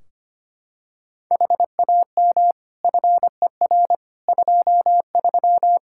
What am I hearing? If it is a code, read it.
Morse code, heard as a keyed tone, about 25 wpm, 700 Hz. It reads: HAM FER 23
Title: Practice QSO - Exchange 3 - More Info - 25wpm